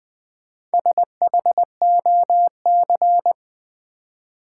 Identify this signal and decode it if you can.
Morse code, heard as a keyed tone, about 20 wpm, 700 Hz. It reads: SHOC
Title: Build A Word - 1 Letter At A Time - 3 to 5-Letter Words - 20wpm